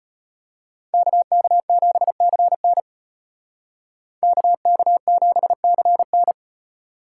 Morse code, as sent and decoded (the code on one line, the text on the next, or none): -.- -.- --... -.-. -. / -.- -.- --... -.-. -.
KK7CN KK7CN